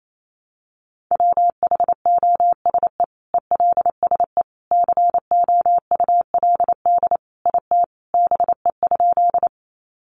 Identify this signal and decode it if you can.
Morse code, heard as a keyed tone, about 28 wpm, 700 Hz. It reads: WHOSE ELSE COULD IT BE?
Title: Sentences from Top 1200 Words 28wpm